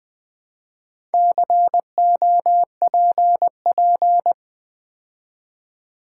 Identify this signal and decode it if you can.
Morse code, heard as a keyed tone, about 20 wpm, 690 Hz. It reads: COPP
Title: Build A Word - 2 Letters At A Time - 6-Letter Words - 20wpm